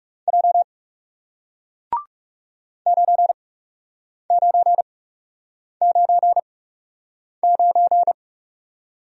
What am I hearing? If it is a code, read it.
Morse code, heard as a keyed tone, about 37 wpm, spaced as for 8 wpm, 700 Hz. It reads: J 9999
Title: Speed-Racing Single Letter-Number 30wpm